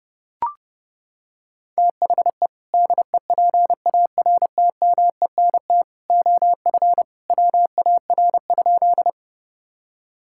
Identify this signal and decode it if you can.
Morse code, heard as a keyed tone, about 30 wpm, 700 Hz. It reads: THE DEPARTMENT OF WAR?
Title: Sentences from Top 1000 Words - Review Part 5 - 30wpm